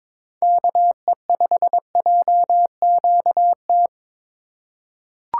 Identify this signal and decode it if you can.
Morse code, heard as a keyed tone, about 22 wpm, 700 Hz. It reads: KE5JQT
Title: USA Call Signs - 6 Characters - 3x Character Spacing - 22wpm